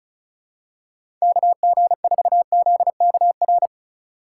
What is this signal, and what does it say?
Morse code, heard as a keyed tone, about 36 wpm, 700 Hz. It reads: KG4ZKR
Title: USA Call Signs - 6 Characters - 2x Character Spacing - 35wpm